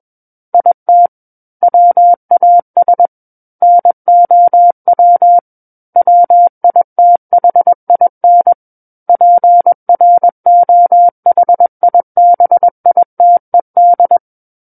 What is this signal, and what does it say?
Morse code, heard as a keyed tone, about 21 wpm, 700 Hz. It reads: IT WAS NOW WITHIN PROHIBITED
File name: Book One / 12-Chapter 12 - 21 WPM